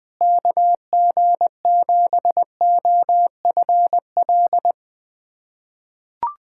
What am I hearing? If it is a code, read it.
Morse code, heard as a keyed tone, about 20 wpm, 700 Hz. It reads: KG7OFL
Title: USA Callsign DE Callsign - 6 Characters - 20wpm